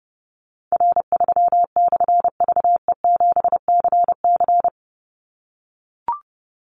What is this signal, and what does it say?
Morse code, heard as a keyed tone, about 30 wpm, 700 Hz. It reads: R3/VE7CC
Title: Call Signs - 30wpm